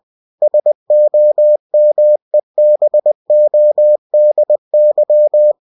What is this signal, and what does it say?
Morse code, heard as a keyed tone, about 20 wpm, 600 Hz. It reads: SOMEBODY
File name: cw-books / winnie-the-pooh / winnie-20wpm-page-0089